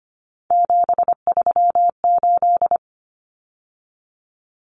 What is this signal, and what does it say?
Morse code, heard as a keyed tone, about 25 wpm, 700 Hz. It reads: 738
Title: Numbers - 3-Digit 25wpm